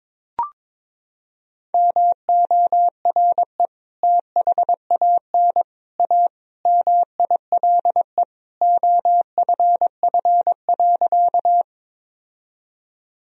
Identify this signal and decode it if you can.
Morse code, heard as a keyed tone, about 22 wpm, 700 Hz. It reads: MORE THAN A MILE OFF.
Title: Sentences from Top 1000 Words - Review Part 1 - 22wpm